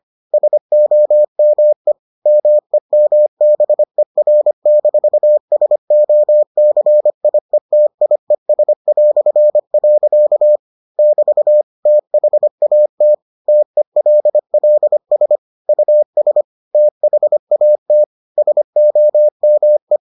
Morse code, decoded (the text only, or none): SOME MEMBER-SOCIETIES". <BT> THAT TELLS US THAT SOME